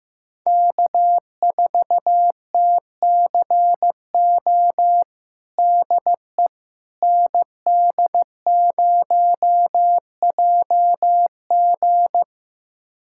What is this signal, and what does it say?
Morse code, heard as a keyed tone, about 15 wpm, 700 Hz. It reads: K4TCO DE ND0JG
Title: USA Callsign DE Callsign - 5 Characters - 15wpm